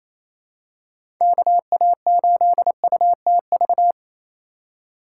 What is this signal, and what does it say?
Morse code, heard as a keyed tone, about 28 wpm, 700 Hz. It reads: KA8UTV
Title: USA Call Signs - 6 Characters - 2x Character Spacing - 28wpm